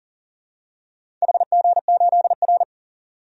Morse code, heard as a keyed tone, about 40 wpm, 700 Hz. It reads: HG8R
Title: DXCC-100 Call Signs - 4 Characters - 40wpm